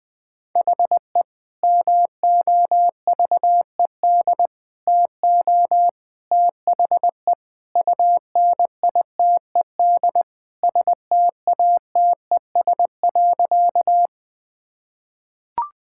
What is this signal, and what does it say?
Morse code, heard as a keyed tone, about 20 wpm, 700 Hz. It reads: HE MOVED TO THE UNITED STATES.
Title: Sentences with 3-Word Phrases - 20wpm